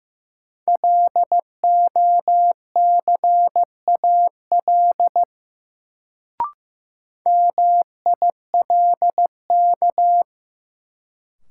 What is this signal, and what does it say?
Morse code, heard as a keyed tone, about 15 wpm, 700 Hz. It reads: LOCAL MILK
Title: Sentences from Top 1500 Words - New Word Drill Alt - 15wpm